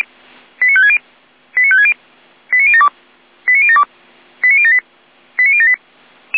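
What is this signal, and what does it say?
Five-tone selective call (ZVEI), heard as quick runs of five tones, tones of about 70 ms. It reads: 89560 89560 89072 89072 89078 89078
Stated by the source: ZVEI five-tone selective call received with an SDR.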